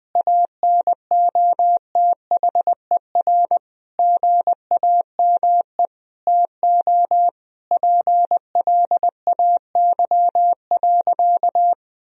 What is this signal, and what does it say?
Morse code, heard as a keyed tone, about 20 wpm, 700 Hz. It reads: ANOTHER GAME TO PLAY.
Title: Sentences from Top 1000 Words - Review Part 3 - 20wpm